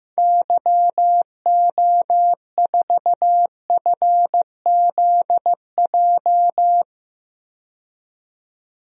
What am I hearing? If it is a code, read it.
Morse code, heard as a keyed tone, about 15 wpm, 700 Hz. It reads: YO4FZJ
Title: DXCC-100 Call Signs - 6 Characters - 15wpm